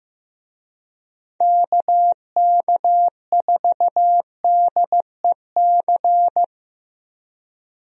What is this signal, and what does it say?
Morse code, heard as a keyed tone, about 15 wpm, 700 Hz. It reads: KK4DEC
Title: USA Call Signs - 6 Characters - Repeated 2x - 15wpm